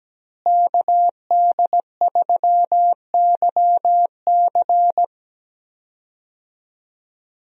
Morse code, heard as a keyed tone, about 17 wpm, 700 Hz. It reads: KD3YC
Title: USA Call Signs - 5 Characters - Repeated 3x - 17wpm